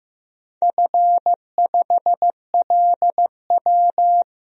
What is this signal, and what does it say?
Morse code, heard as a keyed tone, about 15 wpm, 700 Hz. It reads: F5LW
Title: DXCC-100 Call Signs - 4 Characters - Repeated 3x - 15wpm